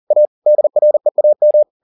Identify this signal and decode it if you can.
Morse code, heard as a keyed tone, about 40 wpm, 600 Hz. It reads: A DREAM
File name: the old man and the sea 2 40wpm